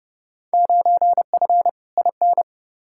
Morse code, heard as a keyed tone, about 30 wpm, 700 Hz. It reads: 9F IN
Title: Field Day Exchanges Encore - 30wpm